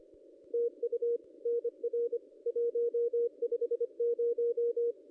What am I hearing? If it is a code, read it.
Morse code, heard as a keyed tone, about 25 wpm, 470 Hz. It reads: TU NR 150
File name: contests / ss-cw-k5tr-op-n5rz -1800